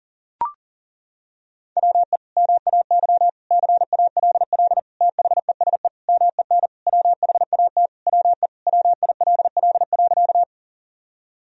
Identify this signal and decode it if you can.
Morse code, heard as a keyed tone, about 40 wpm, 700 Hz. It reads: WE MAY CALL THESE MEN WHAT WE WILL.
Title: Sentences from Top 1000 Words - Review Part 1 - 40wpm